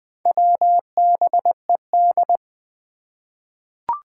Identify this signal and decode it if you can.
Morse code, heard as a keyed tone, about 20 wpm, 700 Hz. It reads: WBED
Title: Four Challenging Letter-Number Combinations 20wpm